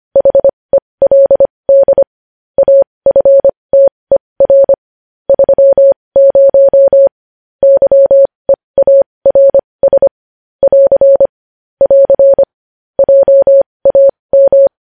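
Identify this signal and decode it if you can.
Morse code, heard as a keyed tone, about 25 wpm, spaced as for 20 wpm, 550 Hz. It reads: HELD AFTER 30 YEARS <AR> <AR> JAM
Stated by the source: Morse code news headlines sent at 20 wpm.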